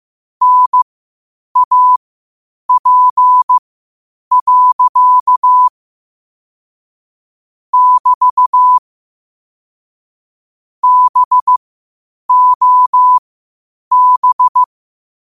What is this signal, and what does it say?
Morse code, heard as a keyed tone, about 15 wpm, spaced as for 8 wpm, 1000 Hz. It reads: NAP. <BT> BOB
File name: qotdfiles / 8wpm